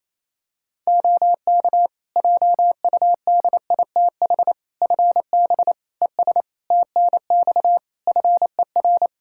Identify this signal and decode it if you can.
Morse code, heard as a keyed tone, about 28 wpm, 700 Hz. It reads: OK JUDITH FB ES TNX FER
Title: Practice QSO - Exchange 2 - Info - 28wpm